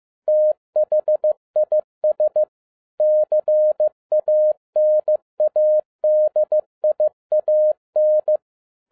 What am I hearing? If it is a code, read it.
Morse code, heard as a keyed tone, about 15 wpm, 610 Hz. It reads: THIS CANADIAN
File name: 2011-10-mp3 / n07-A DOT 15wpm.mp3